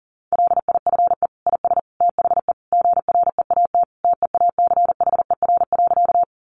Morse code, heard as a keyed tone, about 40 wpm, 700 Hz. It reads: LIFE IS THE GREAT TEACHER.
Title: Sentences from Top 900 Words 40wpm